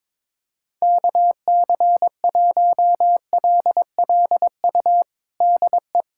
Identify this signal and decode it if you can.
Morse code, heard as a keyed tone, about 21 wpm, 700 Hz. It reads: KC1LLU DE
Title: USA Callsign DE Callsign 22wpm